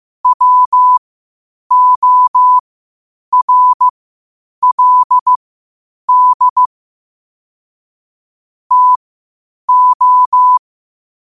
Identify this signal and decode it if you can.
Morse code, heard as a keyed tone, about 15 wpm, spaced as for 8 wpm, 1000 Hz. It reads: WORLD TO
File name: qotdfiles / 8wpm